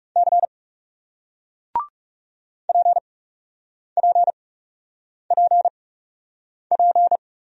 Morse code, heard as a keyed tone, about 39 wpm, spaced as for 8 wpm, 700 Hz. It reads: C PPPP